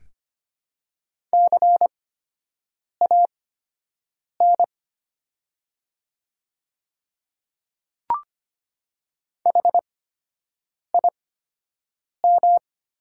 Morse code, heard as a keyed tone, about 25 wpm, spaced as for 7 wpm, 700 Hz. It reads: CAN HIM
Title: 8x Character Spacing - Top 100 Words - 25wpm